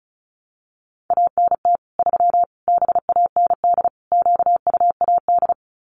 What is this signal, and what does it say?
Morse code, heard as a keyed tone, about 34 wpm, 700 Hz. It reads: ANT 3 BAND QUAD
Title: Common QSO Phrases 35wpm